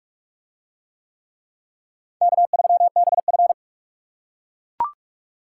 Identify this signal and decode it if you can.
Morse code, heard as a keyed tone, about 38 wpm, 700 Hz. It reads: K3BF